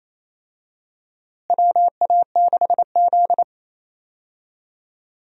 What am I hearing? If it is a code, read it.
Morse code, heard as a keyed tone, about 28 wpm, 700 Hz. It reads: WA6Z
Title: USA Call Signs - 4 Characters - Repeated 3x - 28wpm